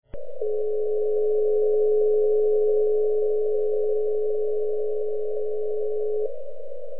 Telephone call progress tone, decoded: dial tone